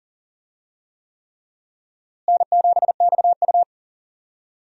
Morse code, heard as a keyed tone, about 40 wpm, 700 Hz. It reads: N7XU
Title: USA Call Signs - 4 Characters - Repeated 3x - 40wpm